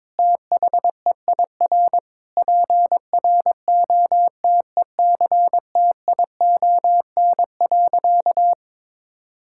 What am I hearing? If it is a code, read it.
Morse code, heard as a keyed tone, about 21 wpm, 700 Hz. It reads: THEIR PROTECTION.
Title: Sentences from Top 1600 Words 22wpm